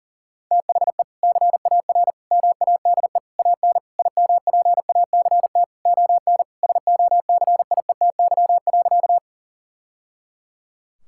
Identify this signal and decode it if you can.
Morse code, heard as a keyed tone, about 40 wpm, 700 Hz. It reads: THE CAR MADE AN IMPACT ON SOCIETY.
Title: Sentences from Top 1000 Words - Review Part 6 - 40wpm